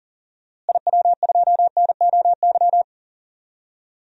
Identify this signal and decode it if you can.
Morse code, heard as a keyed tone, about 40 wpm, 700 Hz. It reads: IW2NOY
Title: DXCC-100 Call Signs - 6 Characters - Repeated 2x - 40wpm